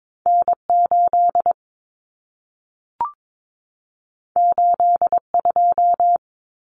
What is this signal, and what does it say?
Morse code, heard as a keyed tone, about 22 wpm, 700 Hz. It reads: N8 82
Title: Common Cut Numbers - 2-Digit 22wpm